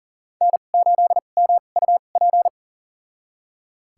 Morse code, heard as a keyed tone, about 38 wpm, spaced as for 28 wpm, 700 Hz. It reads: N8MUP
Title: USA Call Signs - 5 Characters - 2x Character Spacing - 40wpm